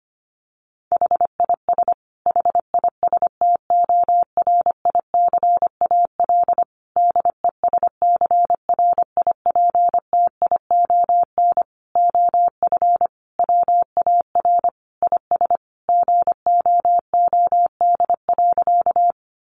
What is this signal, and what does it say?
Morse code, heard as a keyed tone, about 25 wpm, 700 Hz. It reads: HIS HISTORICAL DESCRIPTION OF WAR IS GOOD.
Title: Sentences from Top 1300 Words 25wpm